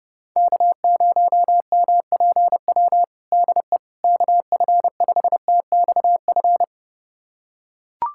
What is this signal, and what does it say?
Morse code, heard as a keyed tone, about 32 wpm, 690 Hz. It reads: K0MPW DE KF5TXF
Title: USA Callsign DE Callsign 30wpm